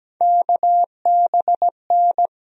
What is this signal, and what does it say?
Morse code, heard as a keyed tone, about 17 wpm, 700 Hz. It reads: KBN
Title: USA Call Sign Postfixes - 3 Letters 17wpm